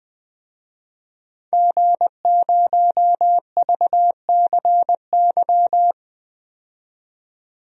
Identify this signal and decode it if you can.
Morse code, heard as a keyed tone, about 20 wpm, 700 Hz. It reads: G0VCY
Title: DXCC-100 Call Signs - 5 Characters - 20wpm